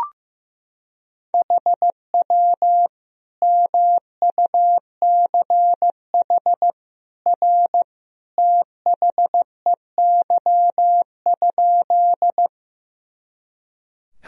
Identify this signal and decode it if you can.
Morse code, heard as a keyed tone, about 15 wpm, 700 Hz. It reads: HW MUCH R THEY?
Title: Sentences from Top 200 Words and Abbreviations 15wpm